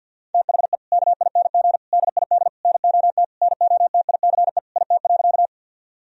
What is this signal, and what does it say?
Morse code, heard as a keyed tone, about 50 wpm, 700 Hz. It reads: THE KING DID NOT NOTICE IT.
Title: Sentences from Top 1000 Words - Review Part 2 - 50wpm